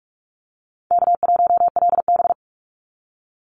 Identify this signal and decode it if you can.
Morse code, heard as a keyed tone, about 46 wpm, 700 Hz. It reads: K1LB
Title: NA SOTA Chaser Callsigns 45wpm